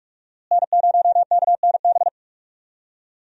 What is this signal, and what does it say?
Morse code, heard as a keyed tone, about 47 wpm, 700 Hz. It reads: N0KNB
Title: USA Call Signs - 5 Characters - Repeated 2x - 45wpm